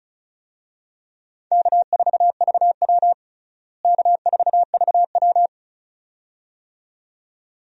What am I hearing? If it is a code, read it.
Morse code, heard as a keyed tone, about 36 wpm, 700 Hz. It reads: K4VW K4VW
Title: USA Call Signs - 4 Characters - Repeated 2x - 35wpm